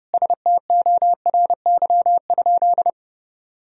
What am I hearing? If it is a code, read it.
Morse code, heard as a keyed tone, about 30 wpm, 700 Hz. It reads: STORY?
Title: Sentences from Top 1000 Words - Review Part 1 - 30wpm